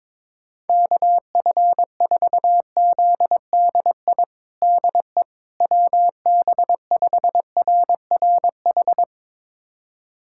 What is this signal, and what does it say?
Morse code, heard as a keyed tone, about 21 wpm, 700 Hz. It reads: KF4ZDI DE WB5RRH
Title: USA Callsign DE Callsign - 6 Characters - 22wpm